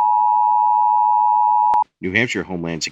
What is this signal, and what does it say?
Emergency Alert System (SAME) tones (the 853 and 960 Hz attention tone), with speech. It is heard from 0.0 to 1.9 s.